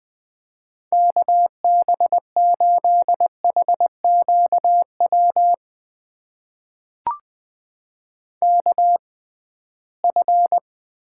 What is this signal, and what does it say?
Morse code, heard as a keyed tone, about 20 wpm, spaced as for 15 wpm, 700 Hz. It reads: KB8HQW KF
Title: USA Call Signs - 6 Characters - 6x Character Spacing - 20wpm